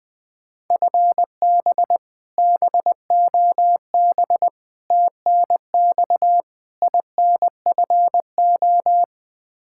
Morse code, heard as a keyed tone, about 20 wpm, 700 Hz. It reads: FB BOB TNX INFO
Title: K1USN SST QSO Practice 20wpm